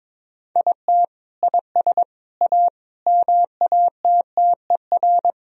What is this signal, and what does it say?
Morse code, heard as a keyed tone, about 22 wpm, 700 Hz. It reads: IT IS A MATTER
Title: Sentences from Top 1000 Words - Review Part 3 - 22wpm